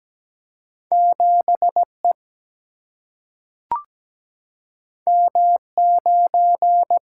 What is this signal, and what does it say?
Morse code, heard as a keyed tone, about 17 wpm, 700 Hz. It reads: 7E M9
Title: Single-Letter Single-Number 17wpm